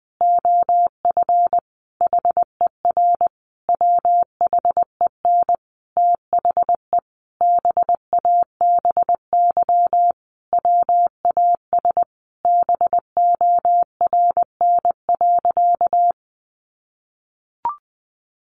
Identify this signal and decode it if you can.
Morse code, heard as a keyed tone, about 20 wpm, 700 Hz. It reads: OF HER WHEN THE BABY WAS BORN.
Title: Sentences from Top 1300 Words 20wpm